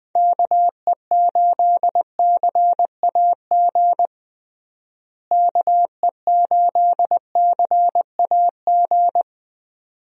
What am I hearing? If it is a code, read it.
Morse code, heard as a keyed tone, about 20 wpm, 700 Hz. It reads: KE8CAG KE8CAG
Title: USA Call Signs - 6 Characters - Repeated 2x - 20wpm